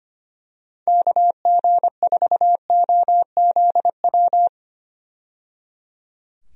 Morse code, heard as a keyed tone, about 25 wpm, 700 Hz. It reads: KG4OZW
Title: USA Call Signs - 6 Characters - 25wpm